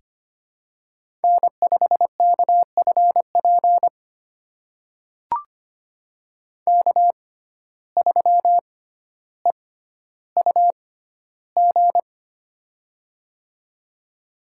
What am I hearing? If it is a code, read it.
Morse code, heard as a keyed tone, about 25 wpm, spaced as for 13 wpm, 700 Hz. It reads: N5KFP K3EUG
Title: USA Call Signs - 5 Characters - 6x Character Spacing - 25wpm